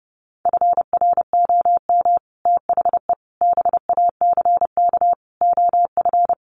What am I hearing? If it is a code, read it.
Morse code, heard as a keyed tone, about 30 wpm, 700 Hz. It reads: FROM THE BACK OF
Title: Sets of 5 Words Encore 30wpm